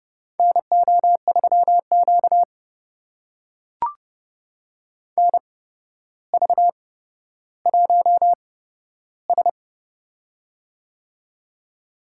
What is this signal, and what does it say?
Morse code, heard as a keyed tone, about 31 wpm, spaced as for 13 wpm, 700 Hz. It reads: NO3Q NV1S
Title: USA Call Signs - 4 Characters - 8x Character Spacing - 30wpm